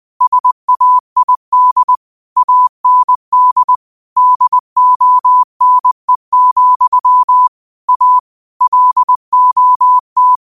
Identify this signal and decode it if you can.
Morse code, heard as a keyed tone, about 20 wpm, 1000 Hz. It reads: SAID AND DONE, A LOT
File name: qotdfiles / 20wpm